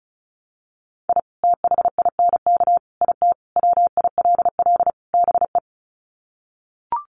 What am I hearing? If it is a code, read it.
Morse code, heard as a keyed tone, about 34 wpm, 700 Hz. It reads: I THINK IT WILL BE
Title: Sets of 5 Words Encore2 35wpm